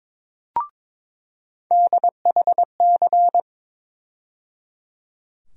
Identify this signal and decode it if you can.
Morse code, heard as a keyed tone, about 21 wpm, 700 Hz. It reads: DHC
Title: USA Call Sign Postfixes - 3 Letters 22wpm